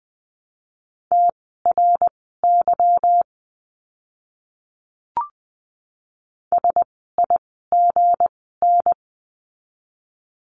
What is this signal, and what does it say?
Morse code, heard as a keyed tone, about 20 wpm, spaced as for 15 wpm, 700 Hz. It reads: TRY SIGN
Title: 160 Lesson 26f - Letter g - Words 20wpm